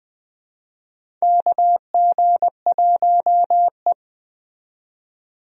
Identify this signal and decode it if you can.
Morse code, heard as a keyed tone, about 20 wpm, 700 Hz. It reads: KG1E
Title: USA Call Signs - 4 Characters - 20wpm